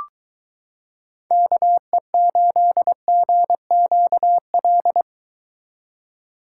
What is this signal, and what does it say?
Morse code, heard as a keyed tone, about 22 wpm, 700 Hz. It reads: KE8GQL